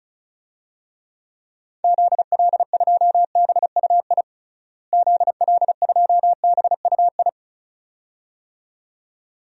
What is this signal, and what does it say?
Morse code, heard as a keyed tone, about 36 wpm, 700 Hz. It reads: ZL2BUI ZL2BUI
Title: DXCC-100 Call Signs - 6 Characters - Repeated 2x - 35wpm